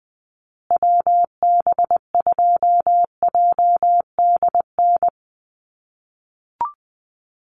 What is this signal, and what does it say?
Morse code, heard as a keyed tone, about 20 wpm, 700 Hz. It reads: WB2JDN
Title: USA Call Signs Repeated 2x 20wpm